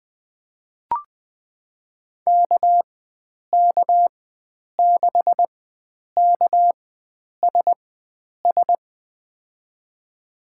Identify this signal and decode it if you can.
Morse code, heard as a keyed tone, about 20 wpm, spaced as for 9 wpm, 700 Hz. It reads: KK6KSS
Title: USA Call Signs - 6 Characters - 4x Character Spacing - 20wpm